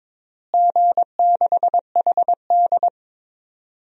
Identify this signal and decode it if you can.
Morse code, heard as a keyed tone, about 22 wpm, 700 Hz. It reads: G6HD
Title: DXCC-100 Call Signs - 4 Characters - Repeated 2x - 22wpm